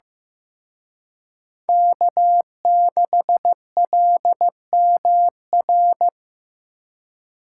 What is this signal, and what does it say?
Morse code, heard as a keyed tone, about 18 wpm, 700 Hz. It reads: K6LMR
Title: Speed Racing - USA Call Signs 15wpm